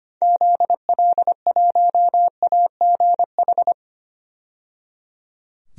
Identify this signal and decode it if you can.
Morse code, heard as a keyed tone, about 25 wpm, 700 Hz. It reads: ZL1AGH
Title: DXCC-100 Call Signs - 6 Characters - 25wpm